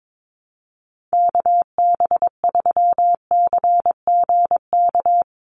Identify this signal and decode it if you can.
Morse code, heard as a keyed tone, about 22 wpm, 700 Hz. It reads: KB3CGK
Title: USA Call Signs 22wpm